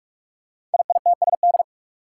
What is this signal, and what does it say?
Morse code, heard as a keyed tone, about 44 wpm, 700 Hz. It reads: IITSD
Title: Code Groups - 5 Letters - 45wpm